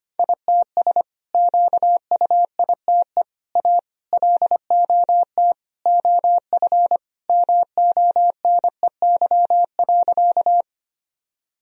Morse code, heard as a keyed tone, about 25 wpm, 700 Hz. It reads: ITS QUITE A LOT OF MONEY.
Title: Sentences from Top 1000 Words - Review Part 3 - 25wpm